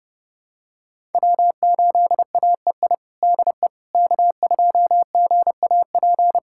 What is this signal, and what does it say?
Morse code, heard as a keyed tone, about 30 wpm, 700 Hz. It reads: W8AEI DE K2GAP
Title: USA Callsign DE Callsign - 5 Characters - 30wpm